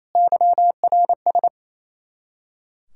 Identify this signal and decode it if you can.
Morse code, heard as a keyed tone, about 34 wpm, 700 Hz. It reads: YRS
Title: Speed-Racing Common QSO Phrases - 28wpm